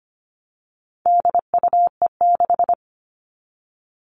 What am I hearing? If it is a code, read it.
Morse code, heard as a keyed tone, about 25 wpm, 700 Hz. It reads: DUE6
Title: All Cut Numbers - 4-Digit 25wpm